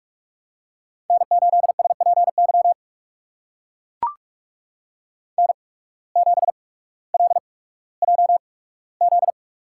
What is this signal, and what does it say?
Morse code, heard as a keyed tone, about 44 wpm, spaced as for 19 wpm, 700 Hz. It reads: N8SPY N7LJZ
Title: USA Call Signs - 5 Characters - 8x Character Spacing - 45wpm